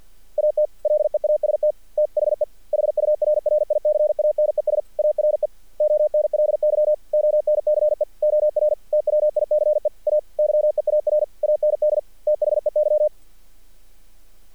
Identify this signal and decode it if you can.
Morse code, heard as a keyed tone, about 48 wpm, 600 Hz. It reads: AT LEAST THE HURRICANES ARE ONLY ONCE OR TWICE A YEAR AND THEY
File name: archived / CRSnet